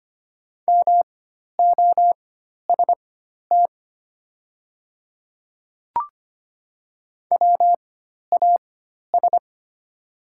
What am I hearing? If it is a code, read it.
Morse code, heard as a keyed tone, about 25 wpm, spaced as for 12 wpm, 700 Hz. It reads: MOST WAS
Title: 4x Character Spacing - Top 100 Words - 25wpm